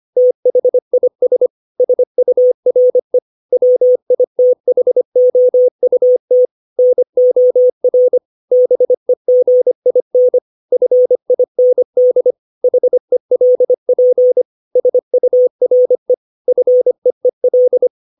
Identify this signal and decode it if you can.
Morse code, heard as a keyed tone, about 25 wpm, 500 Hz. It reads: THIS SURE WITHOUT NOR BEGIN FIND HELP SURE FEEL
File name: 632 words / 632 Words 25WPM